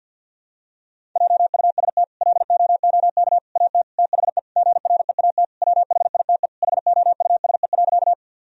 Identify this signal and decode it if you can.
Morse code, heard as a keyed tone, about 49 wpm, 700 Hz. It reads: JUST LOOK AT THE GREAT WHITE HOUSE.
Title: Sentences from Top 1000 Words - Review Part 1 - 50wpm